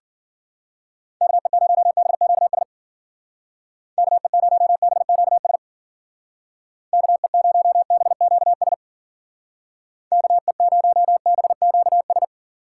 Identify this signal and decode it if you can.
Morse code, heard as a keyed tone, about 50 wpm, 700 Hz. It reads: KE0BQS KE0BQS KE0BQS KE0BQS
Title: Speed Racing - USA 2x3 Call Signs 40wpm